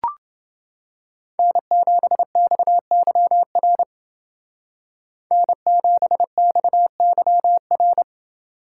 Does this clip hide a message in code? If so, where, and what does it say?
Yes: Morse code from 1.4 to 8.0 s, heard as a keyed tone, about 29 wpm, 700 Hz. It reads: N7XYR N7XYR